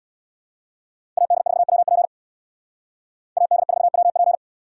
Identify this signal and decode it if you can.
Morse code, heard as a keyed tone, about 63 wpm, 700 Hz. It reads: AD4WP AD4WP